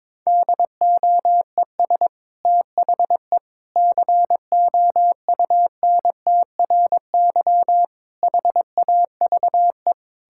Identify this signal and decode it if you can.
Morse code, heard as a keyed tone, about 22 wpm, 700 Hz. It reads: DOES THE COUNTRY HAVE
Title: Sentences from Top 1000 Words - Review Part 5 - 22wpm